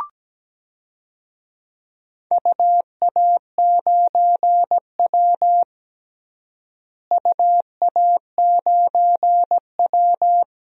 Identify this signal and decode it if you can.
Morse code, heard as a keyed tone, about 17 wpm, 700 Hz. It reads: UA9W UA9W
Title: DXCC-100 Call Signs - 4 Characters - Repeated 2x - 17wpm